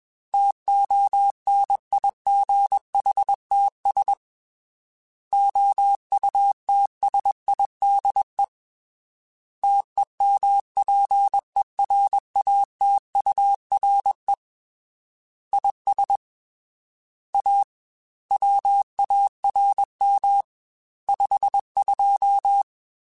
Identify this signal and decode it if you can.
Morse code, heard as a keyed tone, about 21 wpm, 800 Hz. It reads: TONIGHTS OUTSIDE TEMPERATURE IS A WARM 52
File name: Weather Sets / Weather Sets-mp3s / Weather 4 10 @20